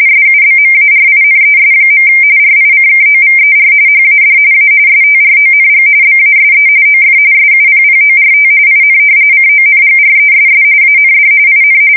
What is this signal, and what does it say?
RTTY, about 45.45 baud. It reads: SUBMARINES. RICKOVER HAD A STRONG BASE OF SUPPORT IN CONGRESS AND IN
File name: fsk170 / usnavy5